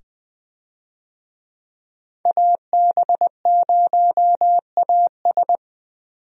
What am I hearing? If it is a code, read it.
Morse code, heard as a keyed tone, about 21 wpm, 700 Hz. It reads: AB0AS
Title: USA Call Signs - 5 Characters - 20wpm